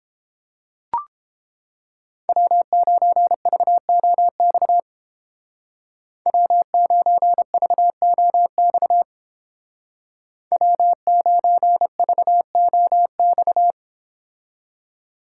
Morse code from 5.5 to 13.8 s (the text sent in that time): W9VOX W9VOX